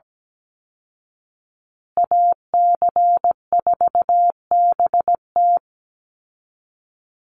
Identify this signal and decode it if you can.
Morse code, heard as a keyed tone, about 17 wpm, 700 Hz. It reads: AC4BT
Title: Call Signs Repeated 2x - 17wpm